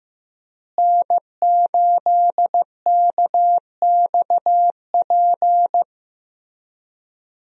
Morse code, heard as a keyed tone, about 15 wpm, 700 Hz. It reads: N8KXP
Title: USA Call Signs - 5 Characters - Repeated 3x - 15wpm